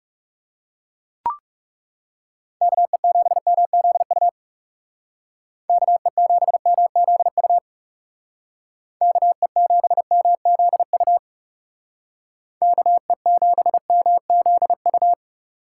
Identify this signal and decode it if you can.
Morse code, heard as a keyed tone, about 37 wpm, 700 Hz. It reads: KE7MZU KE7MZU KE7MZU KE7MZU
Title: Speed Racing - USA 2x3 Call Signs 30wpm